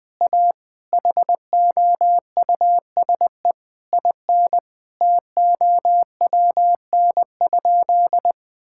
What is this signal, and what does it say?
Morse code, heard as a keyed tone, about 20 wpm, 700 Hz. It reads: A HOUSE IN TOWN?
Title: Sentences from Top 1000 Words - Review Part 6 - 20wpm